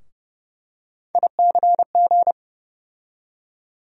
Morse code, heard as a keyed tone, about 30 wpm, 700 Hz. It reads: ICG
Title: USA Call Sign Postfixes - 3 Letters 30wpm